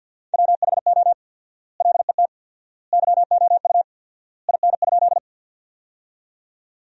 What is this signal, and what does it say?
Morse code, heard as a keyed tone, about 51 wpm, 700 Hz. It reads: WHO LET YOU IN?
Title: 4x Word-Spacing - Sentences from Top 300 Words - 50wpm